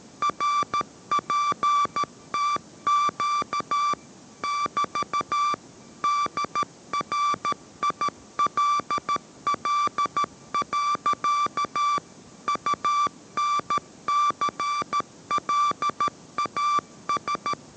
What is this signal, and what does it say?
Morse code, heard as a keyed tone, about 13 wpm, 1170 Hz. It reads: RPTQ <BT> DRILL. UNCLAS